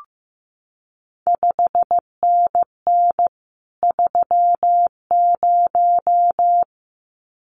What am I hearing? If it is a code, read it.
Morse code, heard as a keyed tone, about 15 wpm, 700 Hz. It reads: 5NN 30
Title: CQ WW DX Contest Exchange Encore - 15wpm